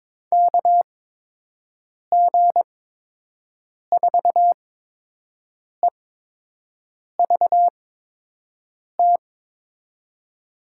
Morse code, heard as a keyed tone, about 22 wpm, spaced as for 6 wpm, 700 Hz. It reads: KG4EVT
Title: USA Call Signs - 6 Characters - 8x Character Spacing - 22wpm